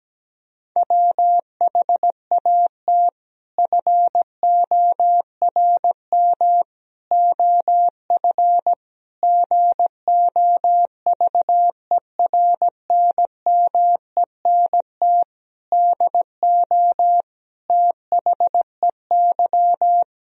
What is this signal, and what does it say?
Morse code, heard as a keyed tone, about 17 wpm, 700 Hz. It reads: WHAT FORM OF GOVERNMENT DO THEY
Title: Sentences from Top 1000 Words - Review Part 3 - 17wpm